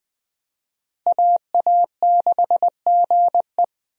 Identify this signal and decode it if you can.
Morse code, heard as a keyed tone, about 20 wpm, 700 Hz. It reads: AA6GE
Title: USA Call Signs - 5 Characters - Repeated 3x - 20wpm